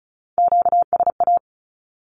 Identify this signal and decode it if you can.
Morse code, heard as a keyed tone, about 36 wpm, 700 Hz. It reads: QSA
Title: Q-Codes with Spoken Meaning - 35wpm